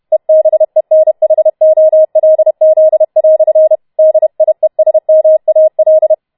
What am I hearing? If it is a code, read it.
Morse code, heard as a keyed tone, about 31 wpm, 630 Hz. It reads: EBENHOLZ" DIESMAL